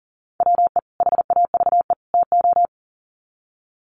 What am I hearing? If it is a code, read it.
Morse code, heard as a keyed tone, about 40 wpm, 700 Hz. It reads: WE HAVE TO
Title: Sets of 3 Words 40wpm